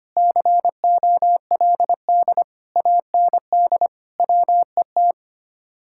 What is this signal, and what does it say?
Morse code, heard as a keyed tone, about 31 wpm, 700 Hz. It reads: COLD AND WET
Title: Speed-Racing Common QSO Phrases - 25wpm